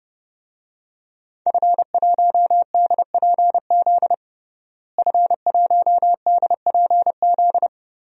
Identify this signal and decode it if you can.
Morse code, heard as a keyed tone, about 32 wpm, 700 Hz. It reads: F1DPZ F1DPZ
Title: DXCC-100 Call Signs - 5 Characters - Repeated 3x - 30wpm